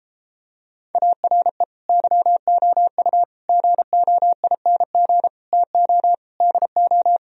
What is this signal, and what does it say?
Morse code, heard as a keyed tone, about 33 wpm, 700 Hz. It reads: ARE YOU GOING TO DO